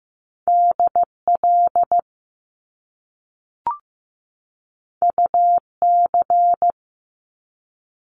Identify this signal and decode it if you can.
Morse code, heard as a keyed tone, about 15 wpm, 700 Hz. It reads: DL UC
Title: Top Two Letter Combinations in Words 15wpm